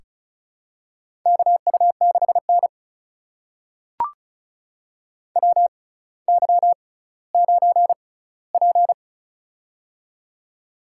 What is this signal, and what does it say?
Morse code, heard as a keyed tone, about 36 wpm, spaced as for 18 wpm, 700 Hz. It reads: KU6N WY9P
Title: USA Call Signs - 4 Characters - 6x Character Spacing - 35wpm